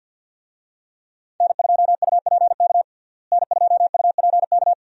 Morse code, heard as a keyed tone, about 52 wpm, 700 Hz. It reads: N2UPK N2UPK
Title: USA Call Signs - 5 Characters - Repeated 3x - 50wpm